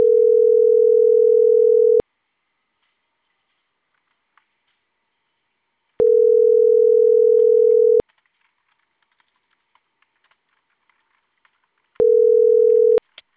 Ringback tone (North America), 440 + 480 Hz.